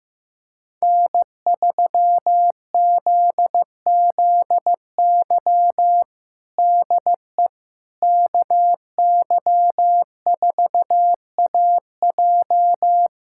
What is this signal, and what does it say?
Morse code, heard as a keyed tone, about 15 wpm, 700 Hz. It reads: N3ZZY DE KY4AJ
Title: USA Callsign DE Callsign - 5 Characters - 15wpm